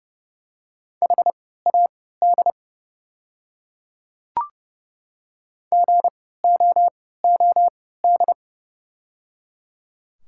Morse code, heard as a keyed tone, about 31 wpm, spaced as for 17 wpm, 690 Hz. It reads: HAD GOOD
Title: 3x Character Spacing - Top 200 Words - 30wpm